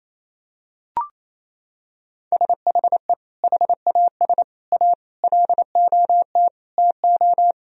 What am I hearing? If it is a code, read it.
Morse code, heard as a keyed tone, about 27 wpm, 700 Hz. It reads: SHE HAS A LOT TO
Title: Sentences from Top 1000 Words - Review Part 3 - 28wpm